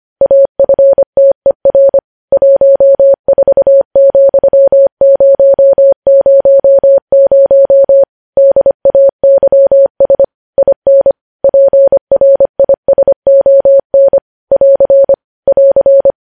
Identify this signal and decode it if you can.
Morse code, heard as a keyed tone, about 25 wpm, 550 Hz. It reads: AFTER 14,000 DAYS IN PRISON <AR> <AR>
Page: News Headlines in Morse Code at 25 WPM